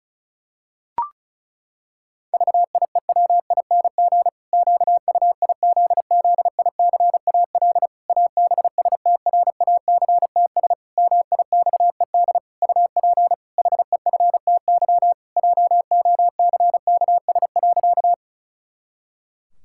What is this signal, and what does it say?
Morse code, heard as a keyed tone, about 34 wpm, 700 Hz. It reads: VIEWING QUIZZICAL ABSTRACTS MIXED UP HEFTY JOCKS.
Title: Pangram Challenge v2 - 35wpm